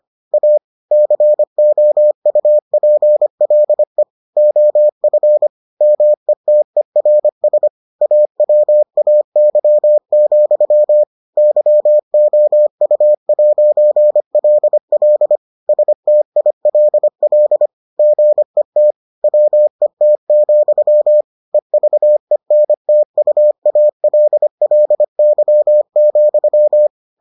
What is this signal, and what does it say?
Morse code, heard as a keyed tone, about 25 wpm, 600 Hz. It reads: A COUPLE OF METERS AWAY, YOU'LL STILL GET WET, EVENTUALLY,